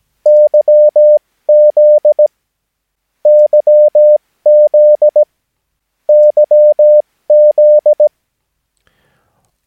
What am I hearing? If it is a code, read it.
Morse code, heard as a keyed tone, about 17 wpm, spaced as for 15 wpm, 600 Hz. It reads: YZ YZ YZ
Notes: Övningsmaterial med alfabetet A till Ö samt siffror 0 till 9 enligt International/Continental Morse Code.